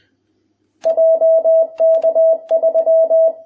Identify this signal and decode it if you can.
Morse code, heard as a keyed tone, about 20 wpm, 660 Hz. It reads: JK3